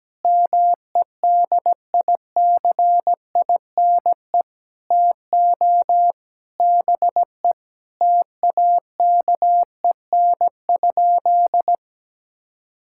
Morse code, heard as a keyed tone, about 17 wpm, 700 Hz. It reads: MEDICINE TO BE TAKEN?
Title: Sentences from Top 1000 Words - Review Part 6 - 17wpm